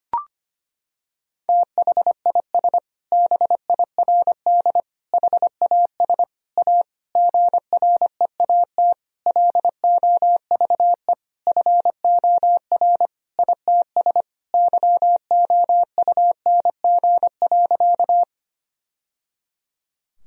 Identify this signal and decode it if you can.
Morse code, heard as a keyed tone, about 25 wpm, 700 Hz. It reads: THIS BIRD HAS A GREAT LOVE FOR ITS YOUNG.
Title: Sentences from Top 1000 Words - Review Part 2 - 25wpm